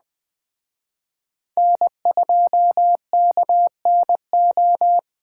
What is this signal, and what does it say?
Morse code, heard as a keyed tone, about 20 wpm, 700 Hz. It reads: N2KNO
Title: Speed Racing - USA 1x3 Call Signs 17wpm